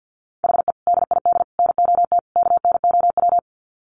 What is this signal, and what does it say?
Morse code, heard as a keyed tone, about 50 wpm, 700 Hz. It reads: HE DID NOT KNOW
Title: Sets of 4 Words 50wpm